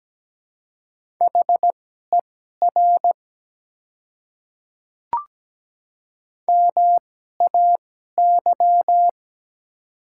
Morse code, heard as a keyed tone, about 17 wpm, spaced as for 12 wpm, 690 Hz. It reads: HER MAY
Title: 2x Character Spacing - Top 100 Words - 17wpm